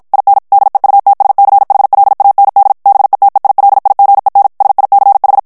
Morse code, heard as a keyed tone, about 54 wpm, 800 Hz. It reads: IN DEUTSCHLAND BETEILIGEN SICH
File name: morse mp3 / ALLEMAND texte clair 2 vitesse 50 mots minute